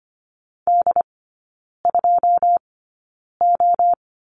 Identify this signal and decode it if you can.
Morse code, heard as a keyed tone, about 25 wpm, spaced as for 9 wpm, 690 Hz. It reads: D2O
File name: Lesson 14e - Number 2 - Three Characters 25wpm